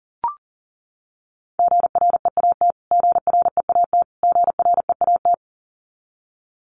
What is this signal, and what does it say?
Morse code, heard as a keyed tone, about 40 wpm, 700 Hz. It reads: GREAT GREAT GREAT
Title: Top 500 Words Repeated 40wpm